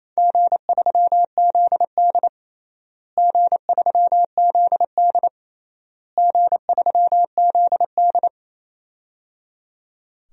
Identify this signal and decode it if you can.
Morse code, heard as a keyed tone, about 28 wpm, 700 Hz. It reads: G3ZD G3ZD G3ZD
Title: DXCC-100 Call Signs - 4 Characters - Repeated 3x - 28wpm